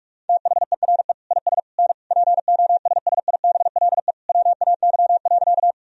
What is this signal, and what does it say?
Morse code, heard as a keyed tone, about 46 wpm, 690 Hz. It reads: THERE IS N POSSIBLE WAY.
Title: Sentences from Top 500 Words and Abbreviations 45wpm